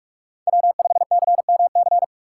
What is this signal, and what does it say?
Morse code, heard as a keyed tone, about 48 wpm, 700 Hz. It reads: W5CMC